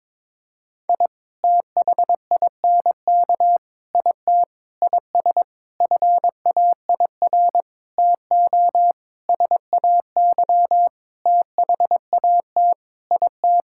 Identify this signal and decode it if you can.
Morse code, heard as a keyed tone, about 22 wpm, 700 Hz. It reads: I THINK IT IS FAIR TO SAY THAT IT
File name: practice / 22wpm / Sentences with 8-Word Phrases - 22wpm